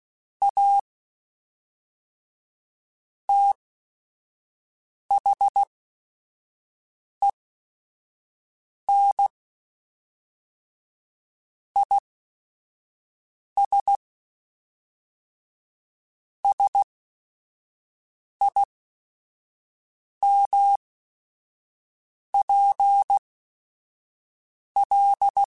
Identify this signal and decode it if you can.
Morse code, heard as a keyed tone, about 16 wpm, spaced as for 5 wpm, 800 Hz. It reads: A THEN IS SIMPL
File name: Technical Subjects / Technical Subjects-mp3s / XC @5